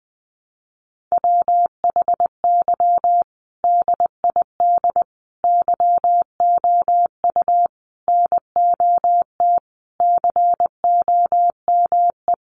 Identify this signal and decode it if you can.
Morse code, heard as a keyed tone, about 20 wpm, 700 Hz. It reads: WHY DID YOU NOT COME
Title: Sentences from Top 1500 Words 20wpm